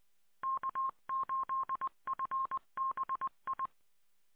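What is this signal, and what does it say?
Morse code, heard as a keyed tone, about 20 wpm, 1060 Hz. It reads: K8FBI
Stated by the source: Morse from k8fbi audio archive.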